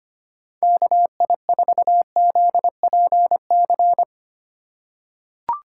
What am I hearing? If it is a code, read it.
Morse code, heard as a keyed tone, about 25 wpm, 700 Hz. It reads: KI4ZPC
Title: USA Call Signs - 6x Character Spacing - 25wpm